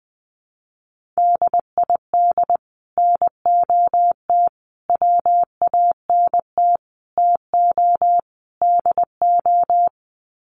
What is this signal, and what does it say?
Morse code, heard as a keyed tone, about 20 wpm, 700 Hz. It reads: DID NOT WANT TO DO
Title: Sets of 5 Words Encore 20wpm